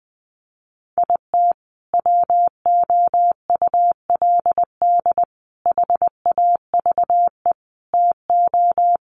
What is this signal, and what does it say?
Morse code, heard as a keyed tone, about 20 wpm, 700 Hz. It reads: IT WOULD HAVE TO
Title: Sets of 5 Words Encore 20wpm